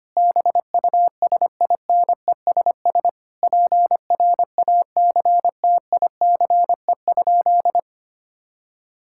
Morse code, heard as a keyed tone, about 25 wpm, 700 Hz. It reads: BUSINESS PRACTICE?
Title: Sentences from Top 1000 Words - Review Part 6 - 25wpm